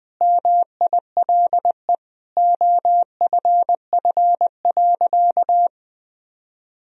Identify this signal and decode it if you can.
Morse code, heard as a keyed tone, about 20 wpm, 700 Hz. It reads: MILE OFF.
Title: Sentences from Top 1000 Words - Review Part 1 - 20wpm